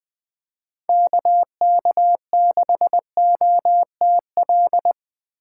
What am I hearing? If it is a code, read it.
Morse code, heard as a keyed tone, about 24 wpm, 700 Hz. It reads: KK6OTL
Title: Speed Racing - USA 2x3 Call Signs 20wpm